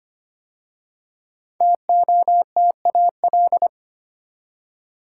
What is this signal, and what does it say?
Morse code, heard as a keyed tone, about 25 wpm, 700 Hz. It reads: TOTAL
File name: 5 Letter Words - First 3 Letters Given - 25wpm